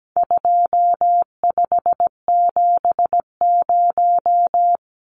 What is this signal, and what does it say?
Morse code, heard as a keyed tone, about 17 wpm, 700 Hz. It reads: 2570
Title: Numbers - 4-Digit 17wpm